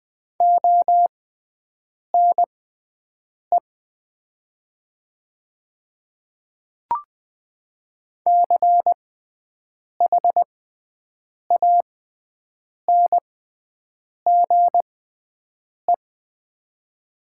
Morse code, heard as a keyed tone, about 20 wpm, spaced as for 7 wpm, 700 Hz. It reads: ONE CHANGE
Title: 6x Character Spacing - Top 300 Words - 20wpm